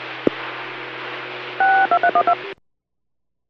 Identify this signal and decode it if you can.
DTMF, heard as telephone touch tones, tones of about 80 ms. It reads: 62312